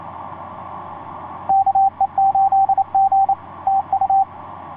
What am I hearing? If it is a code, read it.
Morse code, heard as a keyed tone, about 28 wpm, 780 Hz, with radio noise. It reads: KE8G TU
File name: 6w-gm4fdm f6bee etc 15m cw 30jan2010 1502